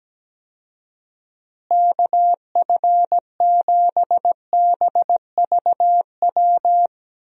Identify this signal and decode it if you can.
Morse code, heard as a keyed tone, about 17 wpm, 700 Hz. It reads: KF7BVW
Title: USA Call Signs - 6 Characters - Repeated 3x - 17wpm